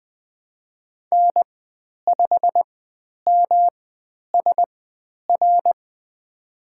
Morse code, heard as a keyed tone, about 20 wpm, spaced as for 10 wpm, 700 Hz. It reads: N5MSR
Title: 109 Lesson 18g - Letter m - Call Signs 20wpm